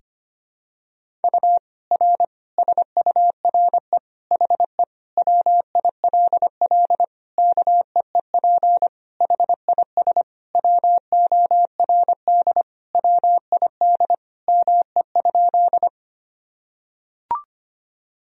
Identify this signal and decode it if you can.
Morse code, heard as a keyed tone, about 25 wpm, 700 Hz. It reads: U R SURE HE WILL KEEP HIS WORD WID ME?
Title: Sentences from Top 300 Words and Abbreviations 25wpm